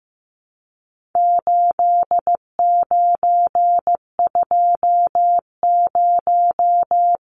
Morse code, heard as a keyed tone, about 15 wpm, 700 Hz. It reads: 8920
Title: Numbers - 4-Digit 15wpm